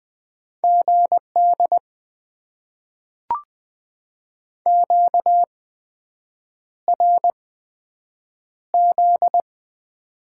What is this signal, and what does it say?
Morse code, heard as a keyed tone, about 20 wpm, spaced as for 7 wpm, 700 Hz. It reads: GD QRZ
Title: Top 90 QSO Elements - 8x Character Spacing - 20wpm